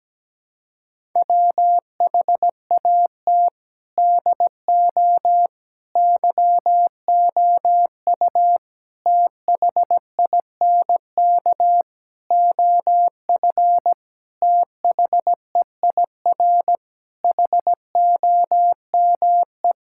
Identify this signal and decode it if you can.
Morse code, heard as a keyed tone, about 17 wpm, 700 Hz. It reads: WHAT DO YOU THINK OF THEIR HOME
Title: Sentences from Top 1000 Words - Review Part 5 - 17wpm